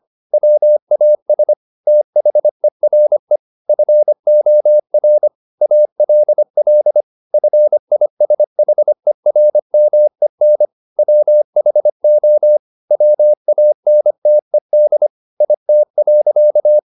Morse code, heard as a keyed tone, about 25 wpm, 600 Hz. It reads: WAS THERE FOR ALL FISHERMEN WHO WANTED IT.
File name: the old man and the sea part 1 25wpm